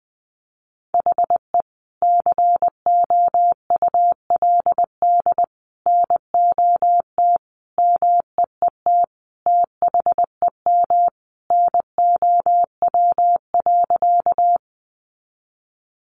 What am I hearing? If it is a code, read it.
Morse code, heard as a keyed tone, about 20 wpm, 700 Hz. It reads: HE COULD NOT MEET THEM NOW.
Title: Sentences from Top 1100 Words 20wpm